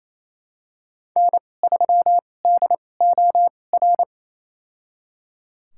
Morse code, heard as a keyed tone, about 28 wpm, spaced as for 20 wpm, 700 Hz. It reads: N3DOR
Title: USA Call Signs - 5 Characters - 2x Character Spacing - 28wpm